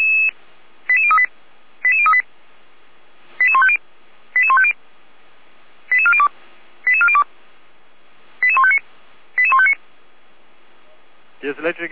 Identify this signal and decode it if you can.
Five-tone selective call (ZVEI), heard as quick runs of five tones, tones of about 70 ms, with speech. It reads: 88038 88038 88150 88150 88402 88402 88159 88159